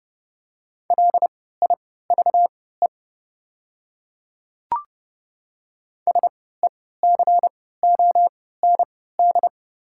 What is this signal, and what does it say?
Morse code, heard as a keyed tone, about 30 wpm, spaced as for 17 wpm, 690 Hz. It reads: LIVE SECOND
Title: 3x Character Spacing - Top 300 Words - 30wpm